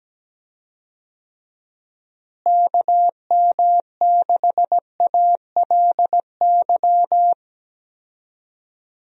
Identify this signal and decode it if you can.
Morse code, heard as a keyed tone, about 17 wpm, 700 Hz. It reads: KM6ALY
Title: USA Call Signs - 6 Characters - Repeated 2x - 17wpm